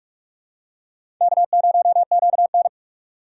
Morse code, heard as a keyed tone, about 46 wpm, 700 Hz. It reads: K0QN
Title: USA Call Signs - 4 Characters - 2x Character Spacing - 45wpm